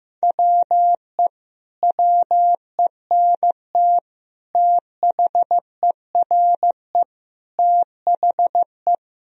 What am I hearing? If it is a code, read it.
Morse code, heard as a keyed tone, about 16 wpm, 700 Hz. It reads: WE WENT THERE THE
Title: Sentences from Top 1000 Words - Review Part 4 - 15wpm